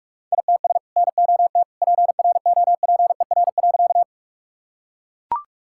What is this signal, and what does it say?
Morse code, heard as a keyed tone, about 43 wpm, 700 Hz. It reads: ITS NOT PROPER.
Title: Sentences from Top 1000 Words - Review Part 5 - 45wpm